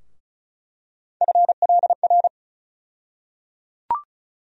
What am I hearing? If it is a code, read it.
Morse code, heard as a keyed tone, about 34 wpm, 700 Hz. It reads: FLR
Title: USA Call Sign Postfixes - 3 Letters 35wpm